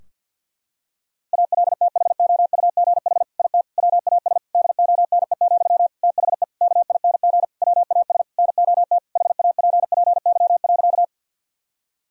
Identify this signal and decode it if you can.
Morse code, heard as a keyed tone, about 50 wpm, 700 Hz. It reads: ALTHOUGH IT WAS DONE, THE KING WAS NOT HAPPY.
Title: Sentences from Top 1000 Words - Review Part 3 - 50wpm